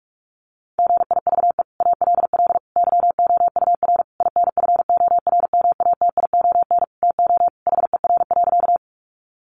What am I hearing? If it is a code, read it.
Morse code, heard as a keyed tone, about 46 wpm, 700 Hz. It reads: GIVE ALL YOUR INFORMATION TO HER.
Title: Sentences from Top 600 Words 45wpm